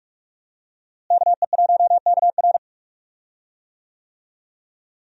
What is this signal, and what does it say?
Morse code, heard as a keyed tone, about 42 wpm, 700 Hz. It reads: KE1KR
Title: USA Call Signs - 5 Characters - Repeated 2x - 45wpm